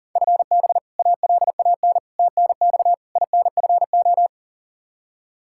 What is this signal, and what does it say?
Morse code, heard as a keyed tone, about 40 wpm, 700 Hz. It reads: FB ALAN TNX INFO
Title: Practice QSO - Exchange 3 - More Info - 40wpm